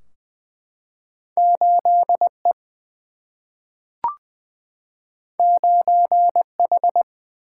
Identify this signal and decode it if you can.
Morse code, heard as a keyed tone, about 20 wpm, 700 Hz. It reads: 8E 9H
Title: Single-Letter Single-Number 20wpm